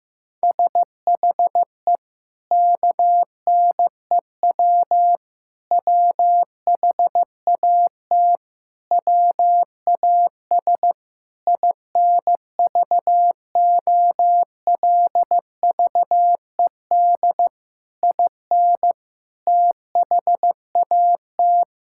Morse code, heard as a keyed tone, about 14 wpm, 700 Hz. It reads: SHE KNEW WHAT WAS INVOLVED IN THAT
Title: Sentences from Top 1000 Words - Review Part 5 - 15wpm